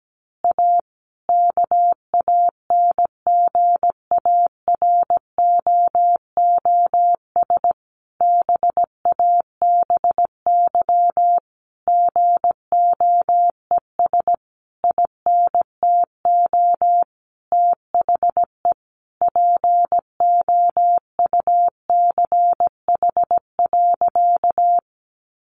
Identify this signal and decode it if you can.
Morse code, heard as a keyed tone, about 17 wpm, 700 Hz. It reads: A KANGAROOS BABY GOES INTO THE POUCH.
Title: Easy Phrases 17wpm